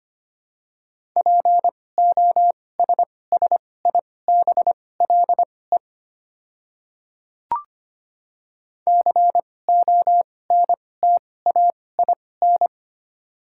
Morse code, heard as a keyed tone, about 25 wpm, spaced as for 18 wpm, 700 Hz. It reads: POSSIBLE CONTAIN
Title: 2x Character Spacing - Top 500 Words - 25wpm